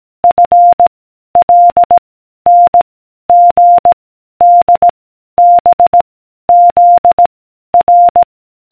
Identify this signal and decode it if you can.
Morse code, heard as a keyed tone, about 18 wpm, 700 Hz. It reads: F L N G D B Z R